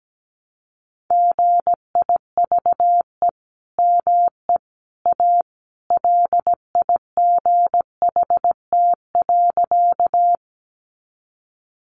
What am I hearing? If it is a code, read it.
Morse code, heard as a keyed tone, about 17 wpm, 700 Hz. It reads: GIVE ME A LIGHT.
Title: Sentences from Top 200 Words 17wpm